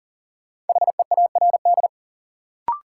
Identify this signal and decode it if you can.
Morse code, heard as a keyed tone, about 40 wpm, 700 Hz. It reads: HEARD
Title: Sentences from Top 400 Words - New Word Drill - 40wpm